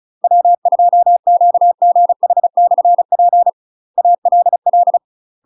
Morse code, heard as a keyed tone, about 36 wpm, 700 Hz. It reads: W2QGH/P ALL
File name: UK MASTER QSOs TRAINING / 35WPM / UK MASTER QSOs 0009